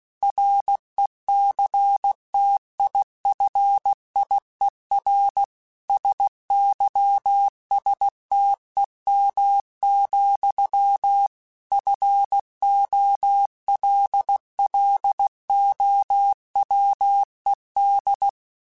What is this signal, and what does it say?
Morse code, heard as a keyed tone, about 16 wpm, 800 Hz. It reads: RECTIFIER SYSTEM, FOLLOWED
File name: Technical Subjects / Technical Subjects-mp3s / Sw Pwr Sup @15